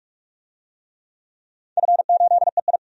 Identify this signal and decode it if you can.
Morse code, heard as a keyed tone, about 45 wpm, 700 Hz. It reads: F8EI
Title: DXCC-100 Call Signs - 4 Characters - 45wpm